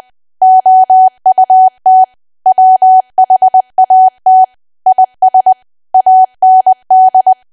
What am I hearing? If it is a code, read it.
Morse code, heard as a keyed tone, about 20 wpm, 750 Hz. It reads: OUT WHAT IS AND
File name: cwa / W1AW-Level3 / 20WPM